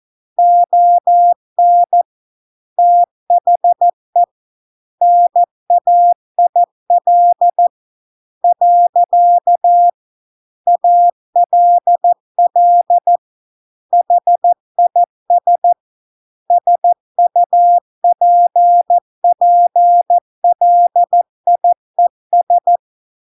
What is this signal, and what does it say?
Morse code, heard as a keyed tone, about 14 wpm, 700 Hz. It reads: ON THE NAIL . ALL HIS SUPPLIES